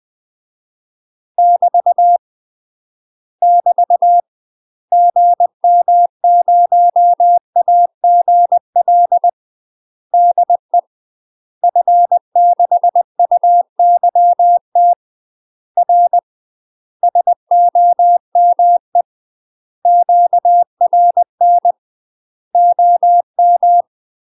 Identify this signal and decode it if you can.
Morse code, heard as a keyed tone, about 20 wpm, 700 Hz. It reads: <BT> <BT> GM0AGL DE F6UYT R SOME QRN OM
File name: UK MASTER QSOs TRAINING / 20WPM / UK MASTER QSOs 0014